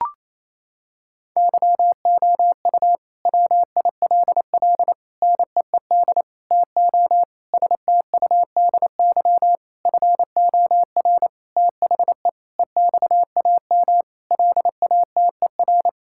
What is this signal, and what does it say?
Morse code, heard as a keyed tone, about 28 wpm, 700 Hz. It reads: YOU WILL NEED TO STUDY FOR THE EXAM LATER
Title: Sentences from Top 1000 Words - Review Part 5 - 28wpm